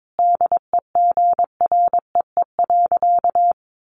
Morse code, heard as a keyed tone, about 22 wpm, 700 Hz. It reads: DEGREE.
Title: Sentences from Top 1500 Words 22wpm